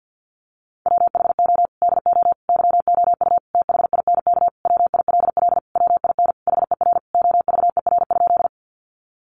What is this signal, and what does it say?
Morse code, heard as a keyed tone, about 50 wpm, 700 Hz. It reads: WHO DO YOU THINK WILL WIN HER OVER?
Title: Sentences from Top 1500 Words 50wpm